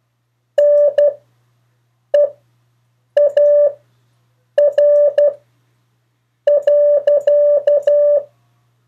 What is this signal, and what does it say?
Morse code, heard as a keyed tone, about 12 wpm, spaced as for 6 wpm, 580 Hz. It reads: NEAR.
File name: 4of4-sentences-mistakes-test